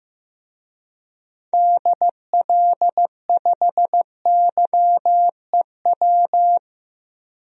Morse code, heard as a keyed tone, about 15 wpm, 700 Hz. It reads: DL5YEW
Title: DXCC-100 Call Signs - 6 Characters - 15wpm